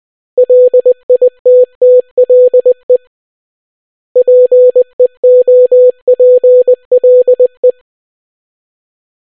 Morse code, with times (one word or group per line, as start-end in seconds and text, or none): 0.4-3.0 LITTLE
4.2-7.7 PEOPLE